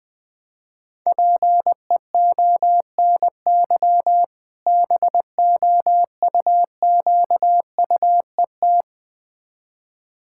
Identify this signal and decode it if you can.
Morse code, heard as a keyed tone, about 21 wpm, 700 Hz. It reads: PEONY BOUQUET
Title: Difficult Words with Hints - 20wpm